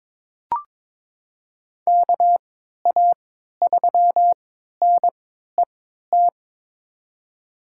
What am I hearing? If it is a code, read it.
Morse code, heard as a keyed tone, about 22 wpm, spaced as for 13 wpm, 700 Hz. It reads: KA3NET
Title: USA Call Signs - 6 Characters - 3x Character Spacing - 22wpm